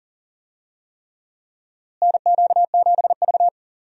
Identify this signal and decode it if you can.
Morse code, heard as a keyed tone, about 40 wpm, 700 Hz. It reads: NQ7V
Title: USA Call Signs - 4 Characters - Repeated 3x - 40wpm